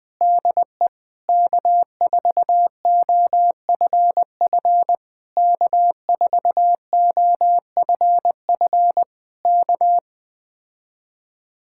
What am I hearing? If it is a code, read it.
Morse code, heard as a keyed tone, about 20 wpm, 700 Hz. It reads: DE K4OFF K4OFF K
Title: Speed-Racing Common QSO Phrases - 17wpm